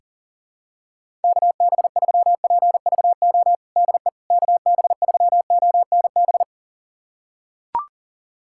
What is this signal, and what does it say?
Morse code, heard as a keyed tone, about 40 wpm, 700 Hz. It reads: KB3PVO DE KB3ONB
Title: USA Callsign DE Callsign - 6 Characters - 40wpm